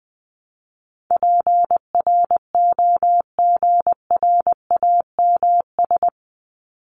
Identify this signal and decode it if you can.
Morse code, heard as a keyed tone, about 20 wpm, 700 Hz. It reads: PROGRAMS
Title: ES:S Suffix - Top 100 Words with Base Words 20wpm